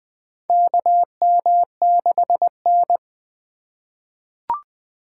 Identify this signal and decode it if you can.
Morse code, heard as a keyed tone, about 20 wpm, 700 Hz. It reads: KM6N
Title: USA Call Signs - 4 Characters - 2x Character Spacing - 20wpm